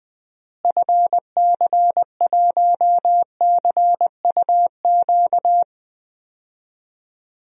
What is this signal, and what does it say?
Morse code, heard as a keyed tone, about 21 wpm, 700 Hz. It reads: FC1CUQ
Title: DXCC-100 Call Signs - 6 Characters - Repeated 3x - 20wpm